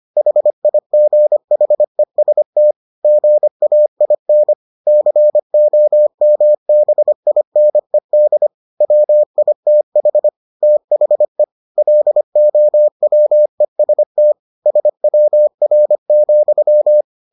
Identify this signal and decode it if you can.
Morse code, heard as a keyed tone, about 25 wpm, 600 Hz. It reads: HIGHEST GAIN COMBINED WITH THE LOWEST SWR,